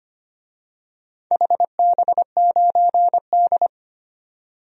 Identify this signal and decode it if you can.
Morse code, heard as a keyed tone, about 25 wpm, 700 Hz. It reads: HB9D
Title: DXCC-100 Call Signs - 4 Characters - 25wpm